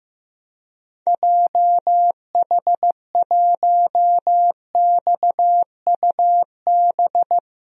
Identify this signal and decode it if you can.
Morse code, heard as a keyed tone, about 15 wpm, 700 Hz. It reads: JH1XUB
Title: DXCC-100 Call Signs - 6 Characters - 15wpm